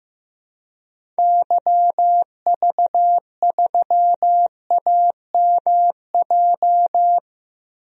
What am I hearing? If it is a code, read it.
Morse code, heard as a keyed tone, about 16 wpm, 700 Hz. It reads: YV3AMJ
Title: DXCC-100 Call Signs - 6 Characters - 15wpm